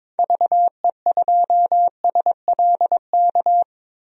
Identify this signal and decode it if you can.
Morse code, heard as a keyed tone, about 22 wpm, 700 Hz. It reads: VE2SLK
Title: DXCC-100 Call Signs - 6 Characters - 22wpm